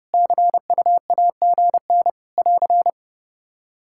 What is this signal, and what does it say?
Morse code, heard as a keyed tone, about 29 wpm, 700 Hz. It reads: CUAGN <AR>
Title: Practice QSO - Exchange 4 - Ending - 30wpm